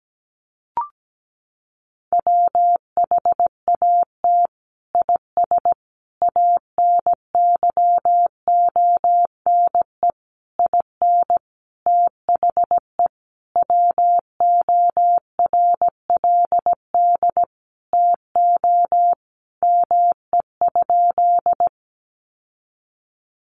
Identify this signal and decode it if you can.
Morse code, heard as a keyed tone, about 17 wpm, 700 Hz. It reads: WHAT IS ANYONE IN THE WORLD TO ME?
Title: Sentences from Top 900 Words 17wpm